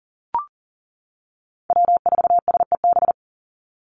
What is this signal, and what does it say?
Morse code, heard as a keyed tone, about 40 wpm, 700 Hz. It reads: W4SEB
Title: USA Call Signs 40wpm